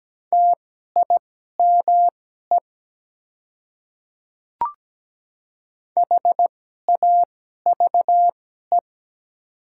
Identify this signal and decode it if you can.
Morse code, heard as a keyed tone, about 17 wpm, spaced as for 12 wpm, 700 Hz. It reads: TIME HAVE
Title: 2x Character Spacing - Top 100 Words - 17wpm